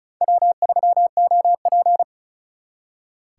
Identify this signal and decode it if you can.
Morse code, heard as a keyed tone, about 45 wpm, 700 Hz. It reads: W3OP